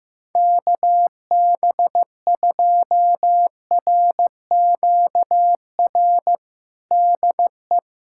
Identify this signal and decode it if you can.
Morse code, heard as a keyed tone, about 15 wpm, 700 Hz. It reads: KB2RQR DE
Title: USA Callsign DE Callsign - 6 Characters - 15wpm